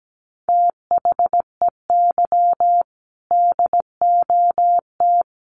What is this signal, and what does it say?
Morse code, heard as a keyed tone, about 17 wpm, 700 Hz. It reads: THEY DOT
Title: Sentences from Top 1500 Words 17wpm